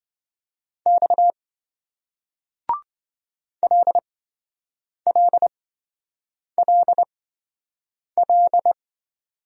Morse code, heard as a keyed tone, about 28 wpm, spaced as for 7 wpm, 700 Hz. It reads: X LLLL